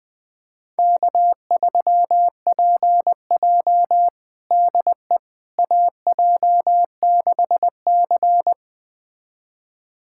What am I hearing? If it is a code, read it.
Morse code, heard as a keyed tone, about 21 wpm, 700 Hz. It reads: K3PJ DE AJ6C
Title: USA Callsign DE Callsign - 4 Characters - 20wpm